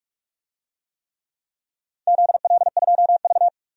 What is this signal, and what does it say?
Morse code, heard as a keyed tone, about 46 wpm, 700 Hz. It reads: ZL2V
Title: DXCC-100 Call Signs - 4 Characters - 45wpm